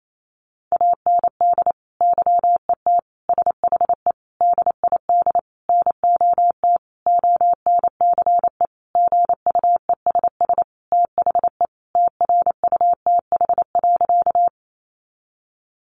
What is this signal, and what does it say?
Morse code, heard as a keyed tone, about 28 wpm, 700 Hz. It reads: AND YET SHE DID NOT ONCE GUESS THE TRUTH.
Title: Sentences from Top 1500 Words 28wpm